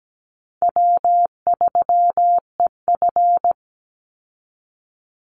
Morse code, heard as a keyed tone, about 17 wpm, 700 Hz. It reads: W3EF
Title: Easy Call Signs Repeated 3x - 17wpm